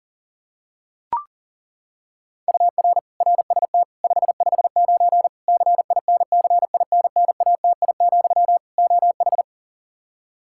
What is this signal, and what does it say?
Morse code, heard as a keyed tone, about 40 wpm, 700 Hz. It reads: UR RST 559 CINCINNATI, OH
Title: Practice QSO - Exchange 1 - Beginning - 40wpm